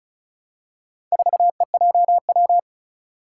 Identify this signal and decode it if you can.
Morse code, heard as a keyed tone, about 36 wpm, 700 Hz. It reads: 4EJW
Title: Four Challenging Letter-Number Combinations 35wpm